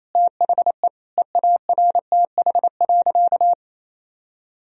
Morse code, heard as a keyed tone, about 27 wpm, 700 Hz. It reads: THE EARTH.
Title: Sentences from Top 1000 Words - Review Part 3 - 28wpm